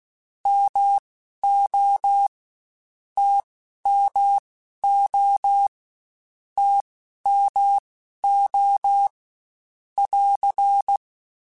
Morse code, heard as a keyed tone, about 16 wpm, spaced as for 11 wpm, 800 Hz. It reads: MO TMO TMO <AR>
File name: Short Runs / Short Runs-mp3s / All DAHs @15